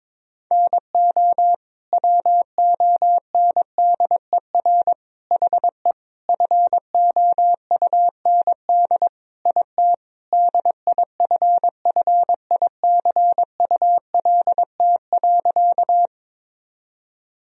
Morse code, heard as a keyed tone, about 22 wpm, 700 Hz. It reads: NO WONDER HE FOUND IT DIFFICULT.
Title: Sentences from Top 1000 Words - Review Part 3 - 22wpm